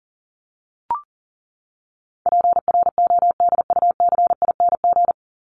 Morse code, heard as a keyed tone, about 40 wpm, 700 Hz. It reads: PRODUCING
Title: ING Suffix - Top 100 Words with Base Words 40wpm